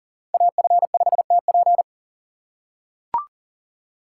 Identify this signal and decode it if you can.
Morse code, heard as a keyed tone, about 40 wpm, 700 Hz. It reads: AF5TP
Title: USA Call Signs - 5 Characters - 4x Character Spacing - 40wpm